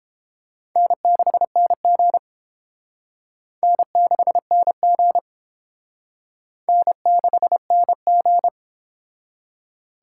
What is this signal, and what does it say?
Morse code, heard as a keyed tone, about 29 wpm, 700 Hz. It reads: N6NG N6NG N6NG